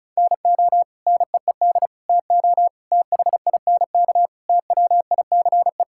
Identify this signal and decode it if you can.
Morse code, heard as a keyed tone, about 34 wpm, 700 Hz. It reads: NO NEED TO THINK TWICE
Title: Sentences from Top 1600 Words 35wpm